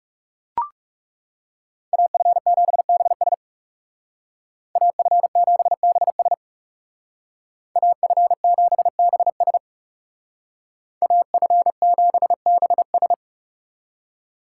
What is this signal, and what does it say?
Morse code, heard as a keyed tone, about 39 wpm, 700 Hz. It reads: AF7BS AF7BS AF7BS AF7BS